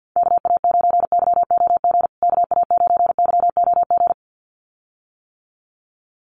Morse code, heard as a keyed tone, about 50 wpm, 700 Hz. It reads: KA9YOG KA9YOG
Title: USA Call Signs Repeated 2x 50wpm